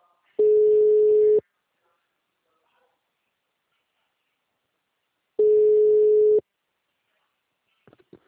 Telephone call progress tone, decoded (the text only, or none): ringback tone